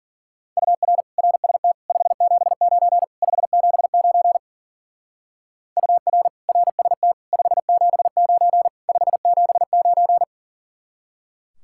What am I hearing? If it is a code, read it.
Morse code, heard as a keyed tone, about 48 wpm, 700 Hz. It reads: UR RST 579 579 UR RST 579 579
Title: Speed-Racing Common QSO Phrases - 40wpm